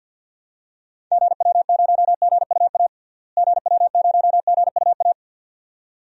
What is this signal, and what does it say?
Morse code, heard as a keyed tone, about 49 wpm, 700 Hz. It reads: GW0GUA GW0GUA
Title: DXCC-100 Call Signs - 6 Characters - Repeated 2x - 50wpm